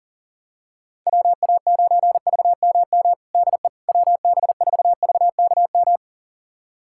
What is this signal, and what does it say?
Morse code, heard as a keyed tone, about 40 wpm, 700 Hz. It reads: WA9VMM DE WB4VKM
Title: USA Callsign DE Callsign - 6 Characters - 40wpm